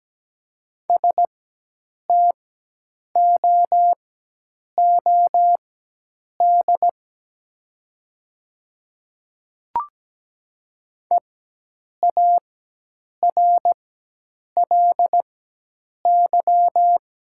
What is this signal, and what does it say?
Morse code, heard as a keyed tone, about 17 wpm, spaced as for 8 wpm, 700 Hz. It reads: STOOD EARLY
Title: 4x Character Spacing - Top 500 Words - 17wpm